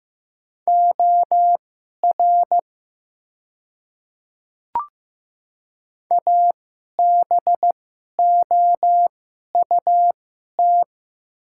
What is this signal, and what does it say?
Morse code, heard as a keyed tone, about 15 wpm, spaced as for 11 wpm, 700 Hz. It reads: OR ABOUT
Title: 2x Character Spacing - Top 100 Words - 15wpm